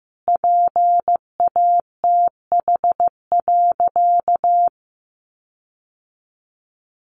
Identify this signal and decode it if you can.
Morse code, heard as a keyed tone, about 15 wpm, 700 Hz. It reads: PATH.
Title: Sentences from Top 1400 Words 15wpm